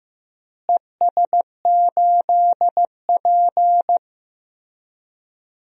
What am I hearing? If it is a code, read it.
Morse code, heard as a keyed tone, about 15 wpm, 700 Hz. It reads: ES8P
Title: DXCC-100 Call Signs - 4 Characters - Repeated 3x - 15wpm